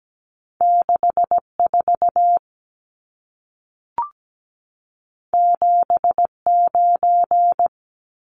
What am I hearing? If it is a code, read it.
Morse code, heard as a keyed tone, about 17 wpm, 700 Hz. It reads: 64 79
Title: Numbers - 2-Digit 17wpm